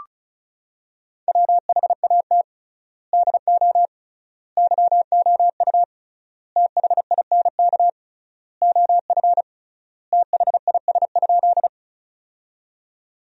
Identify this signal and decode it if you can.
Morse code, heard as a keyed tone, about 34 wpm, 700 Hz. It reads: WHAT DO YOU THINK OF THIS?
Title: 3x Word-Spacing - Sentences from Top 200 Words - 35wpm